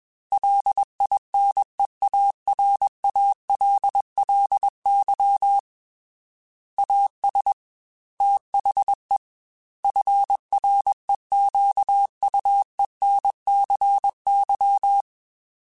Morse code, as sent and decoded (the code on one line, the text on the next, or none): .-.. .. -. .- .-. .- .-.. .-.. -.-- / .- ... / - .... / ..-. .-. --.- ..- . -. -.-. -.--
LINEARALLY AS THE FREQUENCY